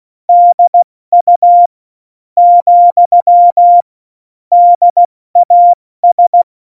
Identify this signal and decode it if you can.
Morse code, heard as a keyed tone, about 16 wpm, 700 Hz. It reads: DU , DAS
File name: Morsekurs / Witze / Witz 22. Mai 2018, 70 F80